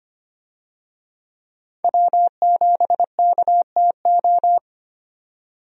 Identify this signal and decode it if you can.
Morse code, heard as a keyed tone, about 25 wpm, 700 Hz. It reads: W7KTO
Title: USA Call Signs - 5 Characters - 25wpm